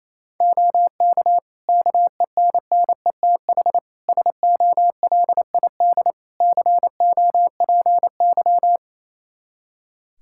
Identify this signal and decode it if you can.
Morse code, heard as a keyed tone, about 28 wpm, 700 Hz. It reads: OK KENNETH SOLID COPY
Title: Practice QSO - Exchange 3 - More Info - 28wpm